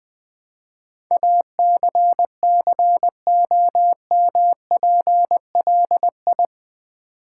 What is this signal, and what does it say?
Morse code, heard as a keyed tone, about 20 wpm, 700 Hz. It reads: ACCOMPLI
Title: Build A Word - 4 Letters At A Time - 10-Letter Words - 20wpm